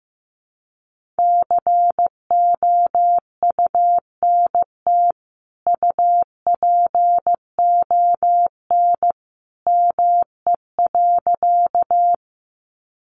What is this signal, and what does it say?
Morse code, heard as a keyed tone, about 15 wpm, 700 Hz. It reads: COUNT UPON ME.
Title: Sentences from Top 1500 Words 15wpm